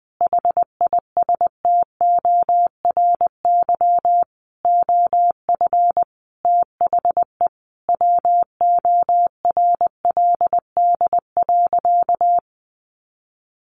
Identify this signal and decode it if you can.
Morse code, heard as a keyed tone, about 19 wpm, 700 Hz. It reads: HISTORY OF THE WORLD.
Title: Sentences from Top 1100 Words 20wpm